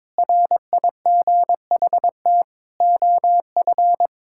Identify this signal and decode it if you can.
Morse code, heard as a keyed tone, about 22 wpm, 700 Hz. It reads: RIGHT OF
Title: Sentences from Top 1000 Words - Review Part 6 - 22wpm